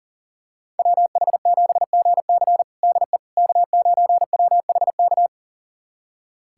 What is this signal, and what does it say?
Morse code, heard as a keyed tone, about 40 wpm, 700 Hz. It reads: WH7GC DE K9WHK
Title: USA Callsign DE Callsign - 5 Characters - 40wpm